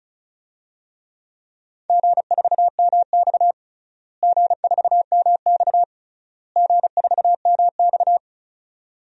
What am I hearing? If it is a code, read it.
Morse code, heard as a keyed tone, about 36 wpm, 700 Hz. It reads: G4MX G4MX G4MX
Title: DXCC-100 Call Signs - 4 Characters - Repeated 3x - 35wpm